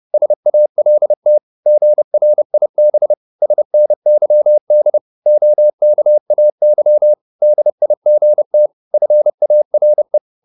Morse code, heard as a keyed tone, about 30 wpm, 600 Hz. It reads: SALT GRIB SNYD OKAY DIGT FARE
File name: Korte ord DK 30wpm